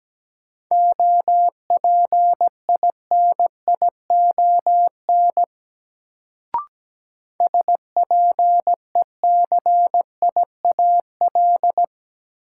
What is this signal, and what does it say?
Morse code, heard as a keyed tone, about 17 wpm, 700 Hz. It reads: OPINION SPECIAL
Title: Sentences from Top 1200 Words - New Word Drill Alt - 17wpm